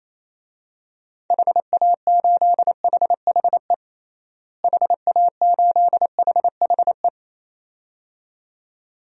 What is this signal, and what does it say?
Morse code, heard as a keyed tone, about 27 wpm, 700 Hz. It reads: HA8HHE HA8HHE
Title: DXCC-100 Call Signs - 6 Characters - Repeated 2x - 28wpm